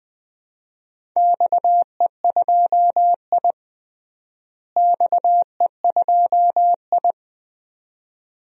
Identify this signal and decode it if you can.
Morse code, heard as a keyed tone, about 21 wpm, 700 Hz. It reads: XE2I XE2I
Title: DXCC-100 Call Signs - 4 Characters - Repeated 2x - 20wpm